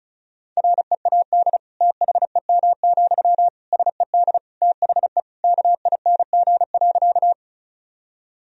Morse code, heard as a keyed tone, about 36 wpm, 700 Hz. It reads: READ THEM, SED THE KING.
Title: Sentences from Top 400 Words and Abbreviations 35wpm